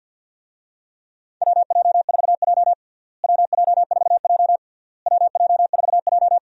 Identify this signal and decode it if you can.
Morse code, heard as a keyed tone, about 50 wpm, 700 Hz. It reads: WJ4J WJ4J WJ4J
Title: USA Call Signs - 4 Characters - Repeated 3x - 50wpm